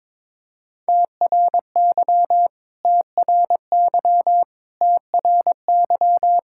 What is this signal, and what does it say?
Morse code, heard as a keyed tone, about 22 wpm, 700 Hz. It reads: TRY TRY TRY
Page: morse-code-ninja-practice-22wpm